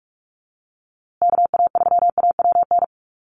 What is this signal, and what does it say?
Morse code, heard as a keyed tone, about 44 wpm, 700 Hz. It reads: KA3AWN
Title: USA Call Signs Repeated 2x 45wpm